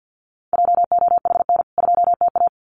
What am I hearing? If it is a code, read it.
Morse code, heard as a keyed tone, about 49 wpm, 700 Hz. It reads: JOHN 2TA
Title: CWT Exchange 50wpm